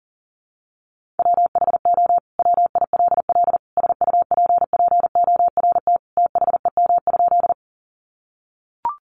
Morse code, heard as a keyed tone, about 40 wpm, 700 Hz. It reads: WHO WILL SUPPORT THEM?
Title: Sentences from Top 800 Words 40wpm